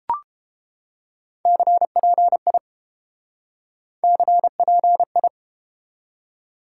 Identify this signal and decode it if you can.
Morse code, heard as a keyed tone, about 33 wpm, 700 Hz. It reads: CPI CPI